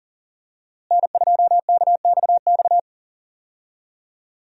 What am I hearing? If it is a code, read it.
Morse code, heard as a keyed tone, about 40 wpm, 700 Hz. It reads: N2KXX
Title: USA Call Signs - 5 Characters - Repeated 3x - 40wpm